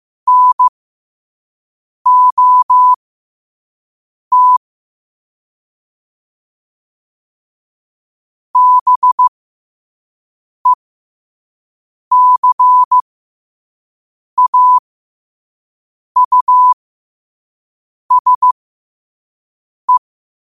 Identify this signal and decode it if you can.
Morse code, heard as a keyed tone, about 15 wpm, spaced as for 5 wpm, 1000 Hz. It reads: NOT BECAUSE
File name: qotdfiles / 5wpm